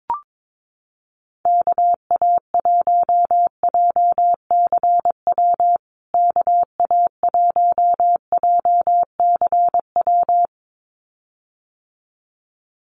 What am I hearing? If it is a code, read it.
Morse code, heard as a keyed tone, about 22 wpm, 700 Hz. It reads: KA1JCW KA1JCW
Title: USA Call Signs Repeated 2x 22wpm